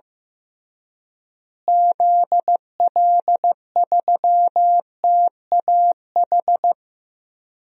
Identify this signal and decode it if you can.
Morse code, heard as a keyed tone, about 15 wpm, 700 Hz. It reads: ZL3TAH
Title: DXCC-100 Call Signs - 6 Characters - Repeated 3x - 15wpm